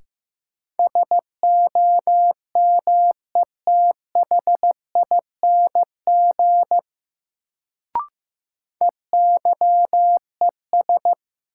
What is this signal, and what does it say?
Morse code, heard as a keyed tone, about 15 wpm, 700 Hz. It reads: SOMETHING EYES
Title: Sets of 5 Words - New Word Drill - 15wpm